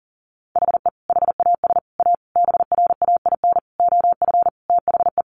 Morse code, heard as a keyed tone, about 40 wpm, 700 Hz. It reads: HE HAS A BRAIN OF THE
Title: Sentences from Top 1100 Words 40wpm